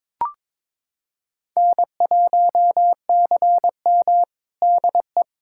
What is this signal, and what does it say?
Morse code, heard as a keyed tone, about 21 wpm, 700 Hz. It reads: N1CM DE
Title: USA Callsign DE Callsign - 4 Characters - 22wpm